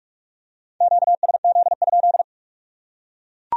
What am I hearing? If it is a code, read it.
Morse code, heard as a keyed tone, about 41 wpm, 700 Hz. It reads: QSZ?